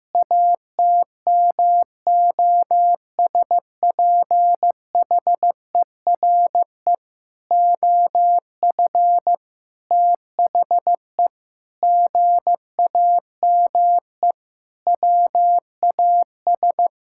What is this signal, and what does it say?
Morse code, heard as a keyed tone, about 15 wpm, 700 Hz. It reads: ATMOSPHERE OF THE GAME WAS
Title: Sentences from Top 1600 Words 15wpm